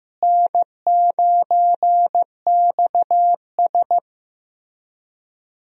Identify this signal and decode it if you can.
Morse code, heard as a keyed tone, about 15 wpm, 700 Hz. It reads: N9XS
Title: USA Call Signs - 4 Characters - Repeated 3x - 15wpm